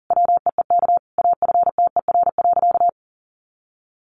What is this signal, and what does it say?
Morse code, heard as a keyed tone, about 40 wpm, 700 Hz. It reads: WEEK AFTER.
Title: Sentences from Top 500 Words 40wpm